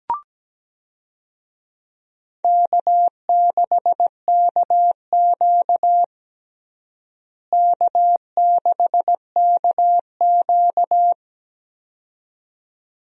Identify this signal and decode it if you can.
Morse code, heard as a keyed tone, about 17 wpm, 700 Hz. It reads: K6KQ K6KQ
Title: USA Call Signs - 4 Characters - Repeated 2x - 17wpm